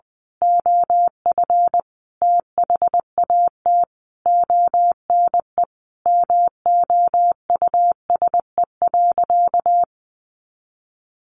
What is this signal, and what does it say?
Morse code, heard as a keyed tone, about 20 wpm, 700 Hz. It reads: OF THAT ONE MOUSE.
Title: A Sound Of Thunder 20wpm